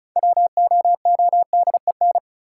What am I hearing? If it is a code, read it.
Morse code, heard as a keyed tone, about 34 wpm, 700 Hz. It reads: WOODEN
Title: Six Letter Words - 35wpm